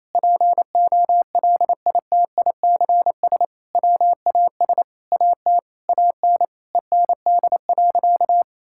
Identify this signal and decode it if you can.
Morse code, heard as a keyed tone, about 28 wpm, 700 Hz. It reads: POLITICS WAS AT AN END.
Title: Sentences from Top 1000 Words - Review Part 6 - 28wpm